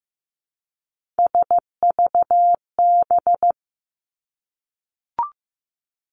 Morse code, heard as a keyed tone, about 15 wpm, 700 Hz. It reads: SVB
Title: Three Challenging Letter-Number Combinations 15wpm